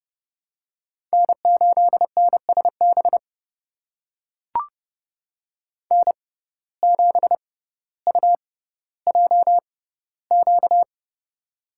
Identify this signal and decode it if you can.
Morse code, heard as a keyed tone, about 31 wpm, spaced as for 16 wpm, 700 Hz. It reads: N8NSB N7UJQ
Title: USA Call Signs - 5 Characters - 6x Character Spacing - 30wpm